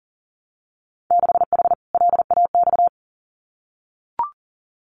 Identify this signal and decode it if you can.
Morse code, heard as a keyed tone, about 38 wpm, 700 Hz. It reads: 6H LAX
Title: Winter Field Day Exchanges Encore 40wpm